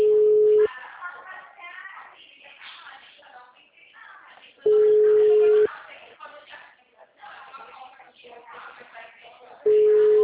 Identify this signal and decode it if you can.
Ringback tone (Europe, 425 Hz), with speech.